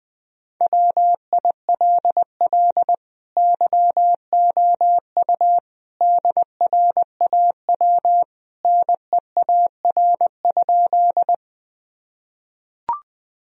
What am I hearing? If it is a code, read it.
Morse code, heard as a keyed tone, about 21 wpm, 700 Hz. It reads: WILL YOU DRAW NEAR?
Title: Sentences from Top 1000 Words - Review Part 1 - 20wpm